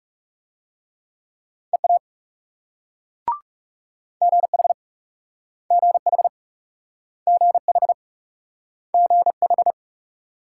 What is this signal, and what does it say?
Morse code, heard as a keyed tone, about 37 wpm, 700 Hz. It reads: EA GH GH GH GH
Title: Speed-Racing Multi-letter Phonemes - 30wpm